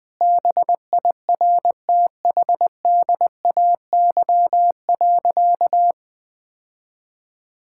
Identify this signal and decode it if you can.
Morse code, heard as a keyed tone, about 20 wpm, 700 Hz. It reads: BIRTHDAY.
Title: Sentences with 6-Word Phrases - 20wpm